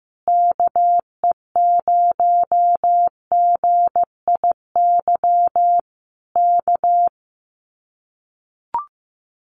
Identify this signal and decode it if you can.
Morse code, heard as a keyed tone, about 16 wpm, 700 Hz. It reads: KE0GIY K
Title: Common QSO Phrases Encore 15wpm